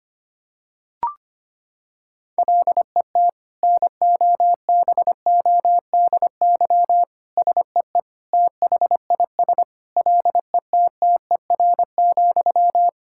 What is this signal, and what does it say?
Morse code, heard as a keyed tone, about 25 wpm, 700 Hz. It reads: LET NOBODY SEE THIS LETTER,
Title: Sentences from Top 1600 Words 25wpm